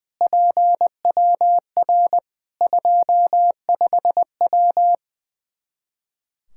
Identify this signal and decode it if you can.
Morse code, heard as a keyed tone, about 20 wpm, 700 Hz. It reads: PWR 25W
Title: K1USN SST QSO Practice 20wpm